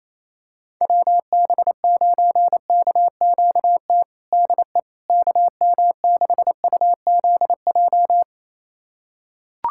Morse code, heard as a keyed tone, about 28 wpm, 700 Hz. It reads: WB9KQT DE KM6UZJ
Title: USA Callsign DE Callsign - 6 Characters - 28wpm